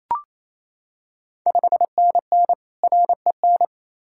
Morse code, heard as a keyed tone, about 28 wpm, 700 Hz. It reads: 5NN REN
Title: Ontario QSO Party Exchange 28wpm